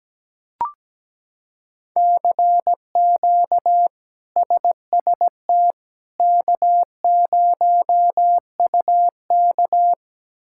Morse code, heard as a keyed tone, about 17 wpm, 700 Hz. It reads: CQ SST K0UK
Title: K1USN SST QSO Practice - 17wpm